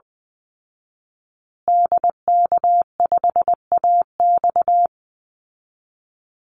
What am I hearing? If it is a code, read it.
Morse code, heard as a keyed tone, about 20 wpm, 700 Hz. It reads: DK5AX
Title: Call Signs Repeated 2x - 20wpm